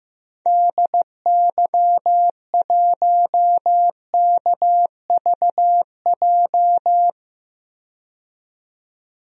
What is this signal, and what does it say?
Morse code, heard as a keyed tone, about 15 wpm, 700 Hz. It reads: DY1KVJ
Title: DXCC-100 Call Signs - 6 Characters - 15wpm